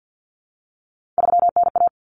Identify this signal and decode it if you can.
Morse code, heard as a keyed tone, about 50 wpm, 700 Hz. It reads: 3NA
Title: Common Cut Numbers - 3-Digit 50wpm